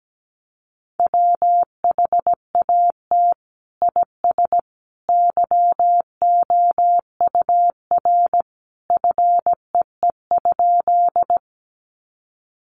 Morse code, heard as a keyed tone, about 17 wpm, 700 Hz. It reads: WHAT IS YOUR FEE?
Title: Sentences from Top 1500 Words 17wpm